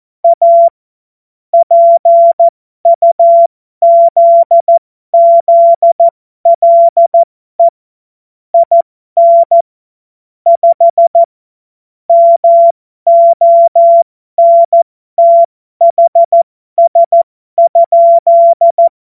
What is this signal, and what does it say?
Morse code, heard as a keyed tone, about 14 wpm, spaced as for 12 wpm, 680 Hz. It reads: A PUZZLE IN 5 MONTHS?
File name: Morsekurs / Witze